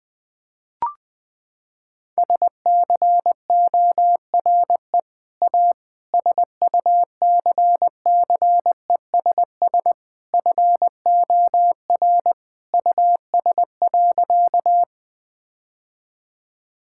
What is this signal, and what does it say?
Morse code, heard as a keyed tone, about 20 wpm, 700 Hz. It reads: SCORE A SUCCESS FOR US.
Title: Sentences from Top 1600 Words 20wpm